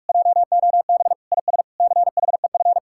Morse code, heard as a keyed tone, about 49 wpm, 700 Hz. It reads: JOB IS CHEF